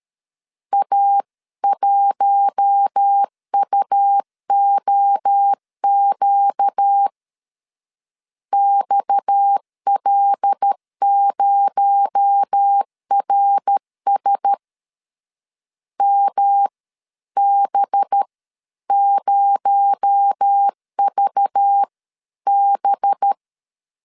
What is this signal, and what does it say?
Morse code, heard as a keyed tone, about 13 wpm, spaced as for 11 wpm, 800 Hz. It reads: A1UOQ XL0RS MB0VB
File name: CW / DET / 1 6 1500C S A 12wpm 3440